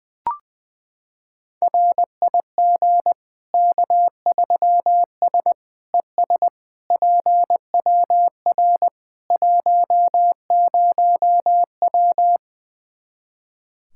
Morse code, heard as a keyed tone, about 21 wpm, 700 Hz. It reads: RIG K3S ES PWR 10W
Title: Practice QSO - Exchange 2 - Info - 20wpm